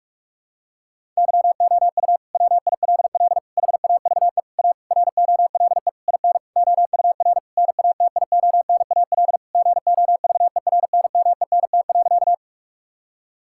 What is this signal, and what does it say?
Morse code, heard as a keyed tone, about 41 wpm, 700 Hz. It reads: YOU WILL HAVE A ROLE IN OUR NATIONAL GOVERNMENT.
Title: Sentences from Top 1000 Words - Review Part 6 - 45wpm